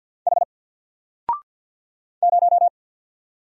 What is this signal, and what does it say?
Morse code, heard as a keyed tone, about 48 wpm, 700 Hz. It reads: H 0
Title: Single Letter-Number Alt 50wpm